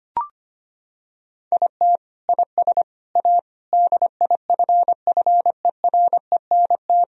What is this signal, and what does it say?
Morse code, heard as a keyed tone, about 25 wpm, 700 Hz. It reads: IT IS A DIFFERENT
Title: Sentences from Top 1000 Words - Review Part 3 - 25wpm